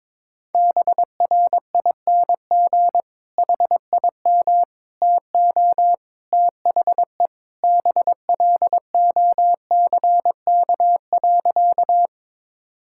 Morse code, heard as a keyed tone, about 22 wpm, 700 Hz. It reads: BRING HIM TO THE BLOCK.
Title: Sentences from Top 1600 Words 22wpm